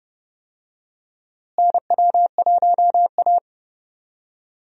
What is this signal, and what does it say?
Morse code, heard as a keyed tone, about 31 wpm, 700 Hz. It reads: NW1A
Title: USA Call Signs - 4 Characters - Repeated 3x - 30wpm